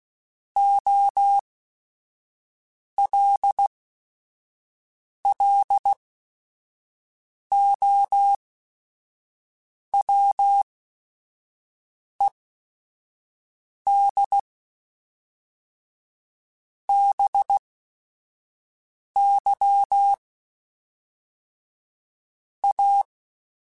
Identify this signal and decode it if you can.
Morse code, heard as a keyed tone, about 16 wpm, spaced as for 5 wpm, 800 Hz. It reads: OLLOWED BY A
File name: Technical Subjects / Technical Subjects-mp3s / Sw Pwr Sup @5